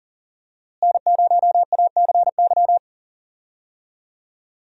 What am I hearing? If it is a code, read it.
Morse code, heard as a keyed tone, about 40 wpm, 700 Hz. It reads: N0ACY
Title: USA Call Signs - 5 Characters - Repeated 2x - 40wpm